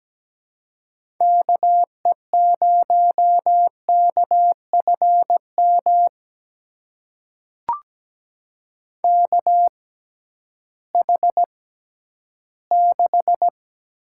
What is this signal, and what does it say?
Morse code, heard as a keyed tone, about 17 wpm, spaced as for 11 wpm, 700 Hz. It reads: KE0KFM KH6
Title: USA Call Signs - 6x Character Spacing - 17wpm